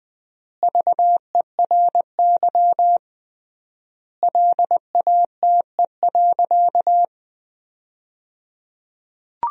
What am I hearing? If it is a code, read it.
Morse code, heard as a keyed tone, about 20 wpm, 700 Hz. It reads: VERY LATE.
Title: 3x Word-Spacing - Sentences from Top 300 Words - 20wpm